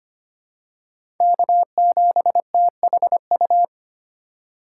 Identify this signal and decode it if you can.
Morse code, heard as a keyed tone, about 25 wpm, 700 Hz. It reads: K7THU
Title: USA Call Signs - 4x Character Spacing - 25wpm